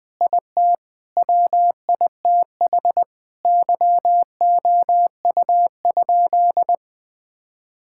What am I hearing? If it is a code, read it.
Morse code, heard as a keyed tone, about 19 wpm, 700 Hz. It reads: IT WITH YOU?
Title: Sentences from Top 1000 Words - Review Part 6 - 20wpm